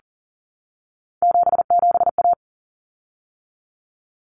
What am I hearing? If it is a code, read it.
Morse code, heard as a keyed tone, about 40 wpm, 700 Hz. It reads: 77A
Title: Common Cut Numbers - 3-Digit 40wpm